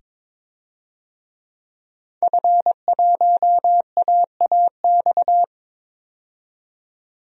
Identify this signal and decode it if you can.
Morse code, heard as a keyed tone, about 22 wpm, 700 Hz. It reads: F1AAX
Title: DXCC-100 Call Signs - 5 Characters - 22wpm